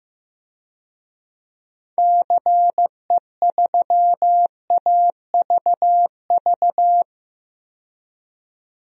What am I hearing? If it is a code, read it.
Morse code, heard as a keyed tone, about 16 wpm, 700 Hz. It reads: CE3AVV
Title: DXCC-100 Call Signs - 6 Characters - 15wpm